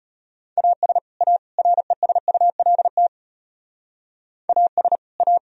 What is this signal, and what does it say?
Morse code, heard as a keyed tone, about 31 wpm, 700 Hz. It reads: AS A RESULT AS A
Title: Speed-Racing - Sets of 3 Words 25wpm